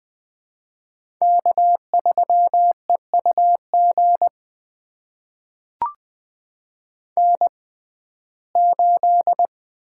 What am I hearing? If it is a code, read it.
Morse code, heard as a keyed tone, about 20 wpm, spaced as for 14 wpm, 700 Hz. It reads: K3EUG N8
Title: USA Call Signs - 5 Characters - 6x Character Spacing - 20wpm